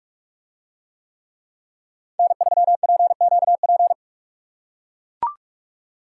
Morse code, heard as a keyed tone, about 40 wpm, 700 Hz. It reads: N3PQP